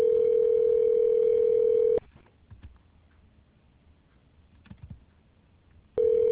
Ringback tone (North America), 440 + 480 Hz.